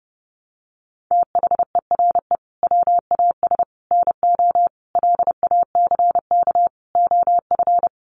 Morse code, heard as a keyed tone, about 30 wpm, 700 Hz. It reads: THERE WAS NO LACK OF
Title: Sentences from Top 1100 Words 30wpm